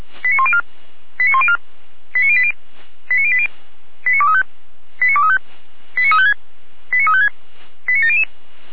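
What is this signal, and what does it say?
Five-tone selective call (ZVEI), heard as quick runs of five tones, tones of about 70 ms. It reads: 89104 89104 89080 89080 89235 89235 89356 89356 89700
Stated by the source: ZVEI five-tone selective call, SDR reception.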